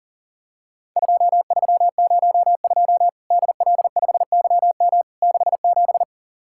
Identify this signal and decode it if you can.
Morse code, heard as a keyed tone, about 43 wpm, 700 Hz. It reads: 2302 DL5YM 67
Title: Worked All Europe DX Contest - QTC Exchange - 40wpm